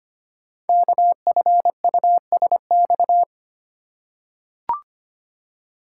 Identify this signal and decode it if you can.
Morse code, heard as a keyed tone, about 25 wpm, 700 Hz. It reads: KFUSX
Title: Code Groups - 5 Letters - 25wpm